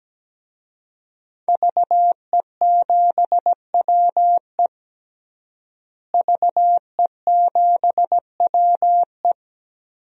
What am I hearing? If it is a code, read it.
Morse code, heard as a keyed tone, about 17 wpm, 700 Hz. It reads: VE7WE VE7WE
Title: DXCC-100 Call Signs - 5 Characters - Repeated 3x - 17wpm